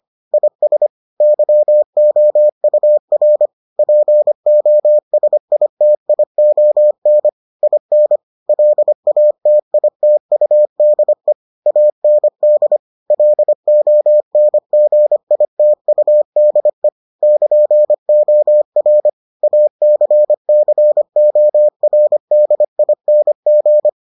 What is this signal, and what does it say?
Morse code, heard as a keyed tone, about 25 wpm, 600 Hz. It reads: IS YOUR POSITION IN LATITUDE AND LONGITUDE <KN>OR ACCORDING